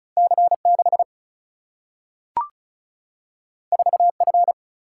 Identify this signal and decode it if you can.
Morse code, heard as a keyed tone, about 36 wpm, 700 Hz. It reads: C 6 4 F
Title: Single-Letter Single-Number 35wpm